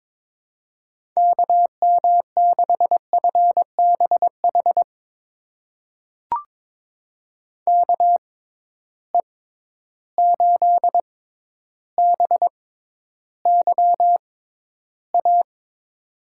Morse code, heard as a keyed tone, about 23 wpm, spaced as for 11 wpm, 700 Hz. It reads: KM6FBH KE8BYA
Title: USA Call Signs - 6 Characters - 6x Character Spacing - 22wpm